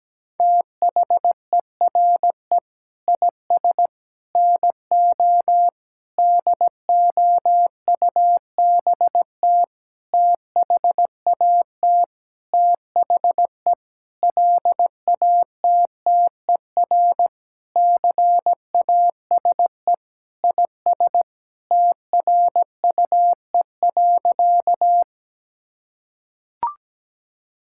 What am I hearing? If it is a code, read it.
Morse code, heard as a keyed tone, about 17 wpm, 700 Hz. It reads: THERE IS NO DOUBT THAT THE LATTER CASE IS TRUE.
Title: Sentences from Top 1600 Words 17wpm